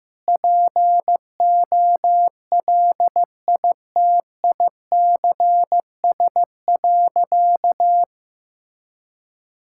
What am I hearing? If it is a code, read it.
Morse code, heard as a keyed tone, about 15 wpm, 700 Hz. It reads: POLITICS.
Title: Sentences from Top 1000 Words - Review Part 6 - 15wpm